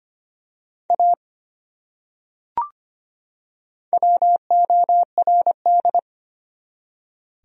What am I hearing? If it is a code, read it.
Morse code, heard as a keyed tone, about 25 wpm, 700 Hz. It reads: A WORD
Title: Top 100 Words 25wpm